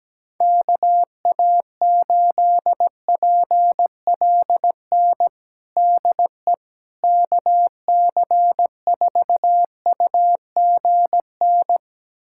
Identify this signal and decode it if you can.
Morse code, heard as a keyed tone, about 17 wpm, 700 Hz. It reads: KA8PLN DE KC4UGN
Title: USA Callsign DE Callsign 17wpm